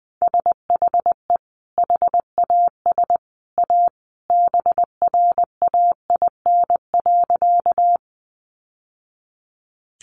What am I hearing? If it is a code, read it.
Morse code, heard as a keyed tone, about 20 wpm, 700 Hz. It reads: SHE HAS A BRAIN.
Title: Sentences from Top 1100 Words 20wpm